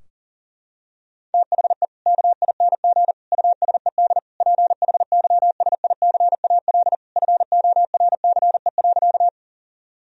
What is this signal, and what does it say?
Morse code, heard as a keyed tone, about 40 wpm, 700 Hz. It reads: THE KING USED PHYSICAL FORCE.
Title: Sentences from Top 1000 Words - Review Part 4 - 40wpm